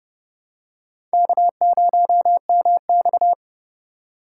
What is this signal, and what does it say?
Morse code, heard as a keyed tone, about 30 wpm, 700 Hz. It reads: K0MX
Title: USA Call Signs - 4x Character Spacing - 30wpm